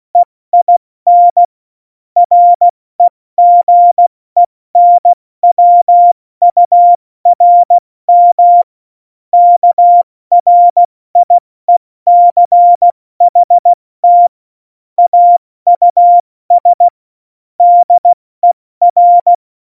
Morse code, heard as a keyed tone, about 16 wpm, 700 Hz. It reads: EIN REGENWURM KRIECHT AUS DER
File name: Morsekurs / Witze / Witz 14. August 2018, 70 F80